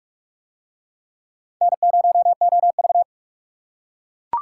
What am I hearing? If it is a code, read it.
Morse code, heard as a keyed tone, about 41 wpm, 700 Hz. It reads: N0OV